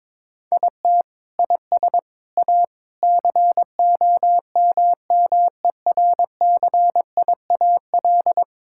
Morse code, heard as a keyed tone, about 22 wpm, 700 Hz. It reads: IT IS A COMMERCIAL
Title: Sentences from Top 1000 Words - Review Part 6 - 22wpm